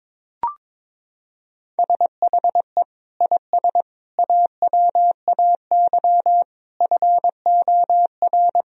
Morse code, heard as a keyed tone, about 22 wpm, 700 Hz. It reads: SHE IS AWAY FOR
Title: Sentences from Top 1000 Words - Review Part 2 - 22wpm